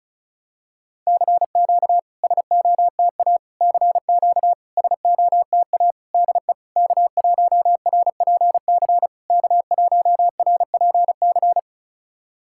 Morse code, heard as a keyed tone, about 36 wpm, 690 Hz. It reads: CQ SOTA CQ SOTA DE K1RPC K1RPC
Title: SOTA QSO Practice 35wpm